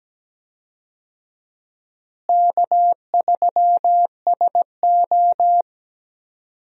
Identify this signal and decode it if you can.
Morse code, heard as a keyed tone, about 17 wpm, 700 Hz. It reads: K3SO
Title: USA Call Signs - 4 Characters - Repeated 3x - 17wpm